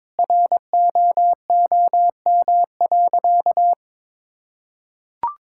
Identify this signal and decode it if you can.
Morse code, heard as a keyed tone, about 22 wpm, 700 Hz. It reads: ROOM.
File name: practice / 22wpm / Sentences with 2-Word Phrases - 22wpm